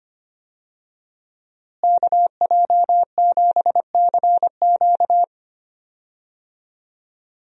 Morse code, heard as a keyed tone, about 25 wpm, 700 Hz. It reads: KJ7CQ
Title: USA Call Signs - 5 Characters - Repeated 3x - 25wpm